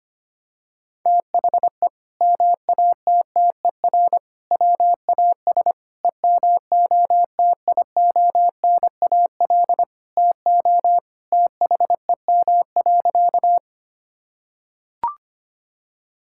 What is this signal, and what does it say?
Morse code, heard as a keyed tone, about 25 wpm, 700 Hz. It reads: THE MATTER WAS EMOTIONAL TO THEM.
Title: Sentences from Top 1000 Words - Review Part 5 - 25wpm